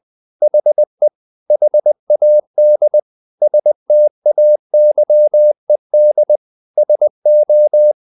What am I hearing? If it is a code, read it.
Morse code, heard as a keyed tone, about 20 wpm, 600 Hz. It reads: HE HAD STAYED SO
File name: the old man and the sea / the old man and the sea part 1 20wpm